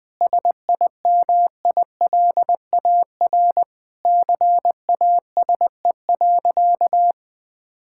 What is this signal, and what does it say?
Morse code, heard as a keyed tone, about 20 wpm, 700 Hz. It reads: SIMILAR CASE.
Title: Sentences from Top 1000 Words - Review Part 5 - 20wpm